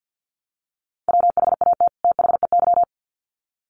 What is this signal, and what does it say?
Morse code, heard as a keyed tone, about 50 wpm, 700 Hz. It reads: WHAT THEY
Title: Sets of 2 Words 50wpm